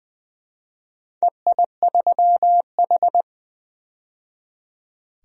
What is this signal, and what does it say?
Morse code, heard as a keyed tone, about 20 wpm, 700 Hz. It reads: EI3H
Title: DXCC-100 Call Signs - 4 Characters - 20wpm